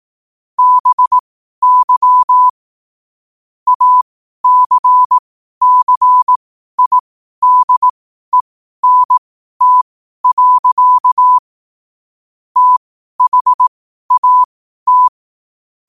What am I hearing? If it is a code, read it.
Morse code, heard as a keyed tone, about 18 wpm, spaced as for 12 wpm, 1000 Hz. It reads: BY ACCIDENT. THAT
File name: qotdfiles / 12wpm